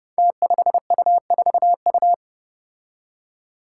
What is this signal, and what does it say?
Morse code, heard as a keyed tone, about 30 wpm, 700 Hz. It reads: T5U4U
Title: Five Challenging Letter-Number Combinations 30wpm